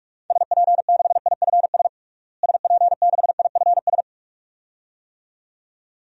Morse code, heard as a keyed tone, about 45 wpm, 700 Hz. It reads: SP6IFS SP6IFS
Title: DXCC-100 Call Signs - 6 Characters - Repeated 2x - 45wpm